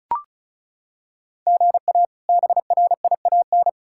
Morse code, heard as a keyed tone, about 36 wpm, 700 Hz. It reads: GA BRIAN
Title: K1USN SST QSO Practice - 35wpm